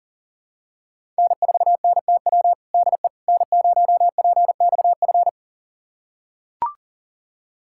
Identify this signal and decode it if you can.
Morse code, heard as a keyed tone, about 40 wpm, 700 Hz. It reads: N4NTW DE N0PXF
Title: USA Callsign DE Callsign - 5 Characters - 40wpm